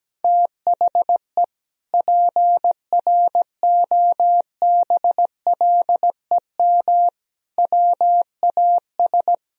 Morse code, heard as a keyed tone, about 17 wpm, 700 Hz. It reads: THE PROBLEM WAS
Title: Sentences from Top 1000 Words - Review Part 2 - 17wpm